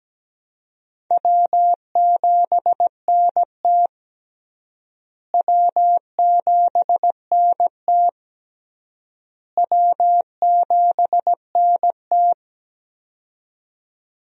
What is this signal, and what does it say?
Morse code, heard as a keyed tone, about 17 wpm, 700 Hz. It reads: W7NT W7NT W7NT
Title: USA Call Signs - 4 Characters - Repeated 3x - 17wpm